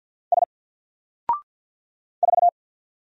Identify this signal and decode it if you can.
Morse code, heard as a keyed tone, about 48 wpm, 700 Hz. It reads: S 4
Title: Single Character (Letter:Number:Punctuation:Prosign) Alt 50wpm